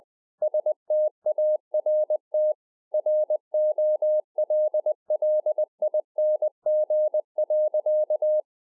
Morse code, heard as a keyed tone, about 20 wpm, 610 Hz. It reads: START ROLLING.